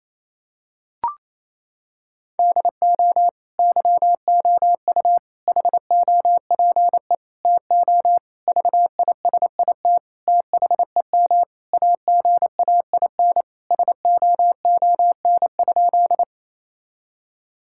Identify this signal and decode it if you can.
Morse code, heard as a keyed tone, about 28 wpm, 700 Hz. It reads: DO YOU HOPE TO VISIT THEM AGAIN SOON?
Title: Sentences from Top 1000 Words - Review Part 6 - 28wpm